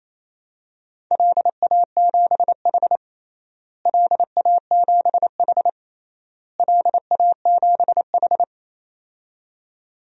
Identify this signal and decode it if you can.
Morse code, heard as a keyed tone, about 28 wpm, 700 Hz. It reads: LA7H LA7H LA7H
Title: DXCC-100 Call Signs - 4 Characters - Repeated 3x - 28wpm